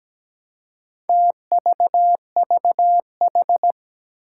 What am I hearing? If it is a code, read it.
Morse code, heard as a keyed tone, about 17 wpm, 700 Hz. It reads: TVVH
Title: Four Challenging Letter-Number Combinations 17wpm